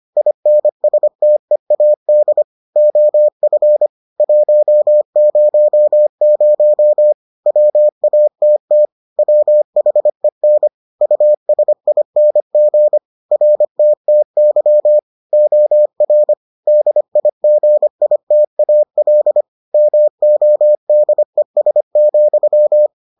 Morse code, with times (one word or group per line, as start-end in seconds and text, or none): 0.2-2.4 INSTEAD
2.8-3.9 OF
4.2-7.1 100
7.5-8.9 WATT
9.2-10.7 WHEN
11.0-13.0 USING
13.3-15.0 RTTY
15.3-16.3 OR
16.7-19.4 DIGITAL
19.7-22.9 MODES,